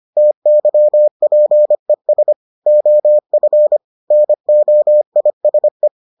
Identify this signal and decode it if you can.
Morse code, heard as a keyed tone, about 25 wpm, 600 Hz. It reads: TYPES OF NOISE